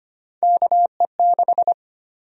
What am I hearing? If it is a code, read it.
Morse code, heard as a keyed tone, about 25 wpm, 700 Hz. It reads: KE6
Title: USA Call Sign Prefixes - 25wpm